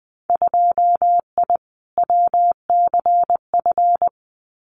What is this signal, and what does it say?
Morse code, heard as a keyed tone, about 20 wpm, 700 Hz. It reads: 2I WCF
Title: Winter Field Day Exchanges 20wpm